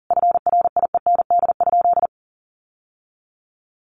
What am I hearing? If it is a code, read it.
Morse code, heard as a keyed tone, about 37 wpm, 700 Hz. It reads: FRIEND?
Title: Sentences from Top 1000 Words 40wpm